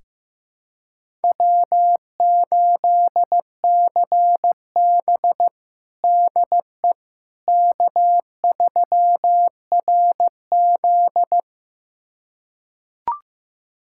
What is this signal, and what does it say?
Morse code, heard as a keyed tone, about 15 wpm, 700 Hz. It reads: W8CB DE K3RZ
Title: USA Callsign DE Callsign - 4 Characters - 15wpm